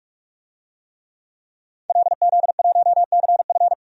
Morse code, heard as a keyed tone, about 41 wpm, 700 Hz. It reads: LZ1CF
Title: DXCC-100 Call Signs - 5 Characters - 45wpm